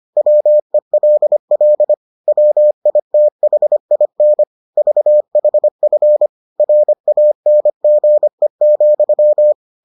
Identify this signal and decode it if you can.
Morse code, heard as a keyed tone, about 25 wpm, 600 Hz. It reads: WELL WITHIN VHF RANGE,